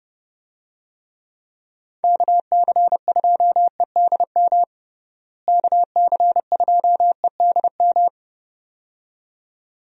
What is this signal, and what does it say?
Morse code, heard as a keyed tone, about 30 wpm, 700 Hz. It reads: KC2EDM KC2EDM
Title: USA Call Signs - 6 Characters - Repeated 2x - 30wpm